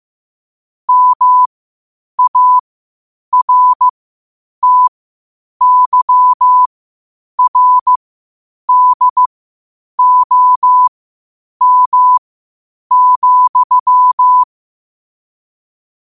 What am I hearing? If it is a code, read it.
Morse code, heard as a keyed tone, about 15 wpm, spaced as for 8 wpm, 1000 Hz. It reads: MARTYRDOM,
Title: 8 WPM morse code quotes for Fri, 01 Mar 2024